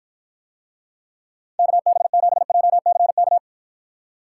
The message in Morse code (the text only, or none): KB7JCK